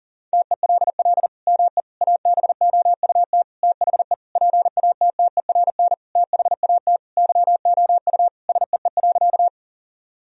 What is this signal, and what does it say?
Morse code, heard as a keyed tone, about 40 wpm, 700 Hz. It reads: TELL ME ABOUT THE PATTERN THAT YOU SEE.
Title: Sentences from Top 1000 Words - Review Part 2 - 40wpm